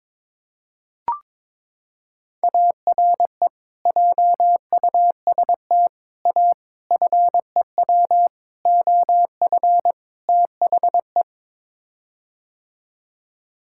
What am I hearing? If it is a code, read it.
Morse code, heard as a keyed tone, about 22 wpm, 700 Hz. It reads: ARE JUST A FEW OF THE
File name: practice / 22wpm / Sets of 6 Words - 22wpm